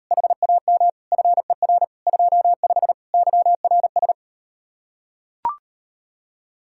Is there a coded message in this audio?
Yes: Morse code from 0.1 to 4.1 s, heard as a keyed tone, about 39 wpm, 700 Hz. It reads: HAM FER 25 YRS